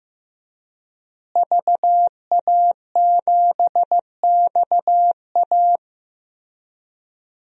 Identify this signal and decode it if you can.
Morse code, heard as a keyed tone, about 15 wpm, 700 Hz. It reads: VA7XA
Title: DXCC-100 Call Signs - 5 Characters - Repeated 3x - 15wpm